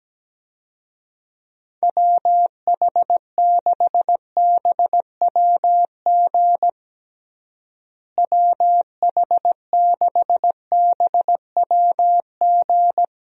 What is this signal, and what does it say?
Morse code, heard as a keyed tone, about 17 wpm, 700 Hz. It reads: WH6BWG WH6BWG
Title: USA Call Signs - 6 Characters - Repeated 3x - 17wpm